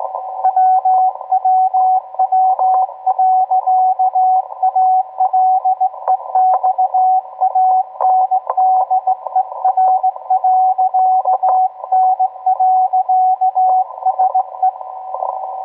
Morse code, heard as a keyed tone, about 18 wpm, 750 Hz, with radio noise. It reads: WWW.ALEXANDER.N.SE